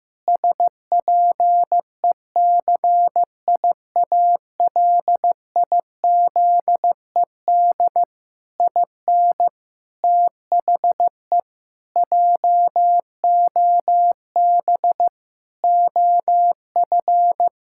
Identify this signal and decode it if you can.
Morse code, heard as a keyed tone, about 16 wpm, 700 Hz. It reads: SPECIALIZED IN THE JOB OF
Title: Pangram Challenge v2 - 15wpm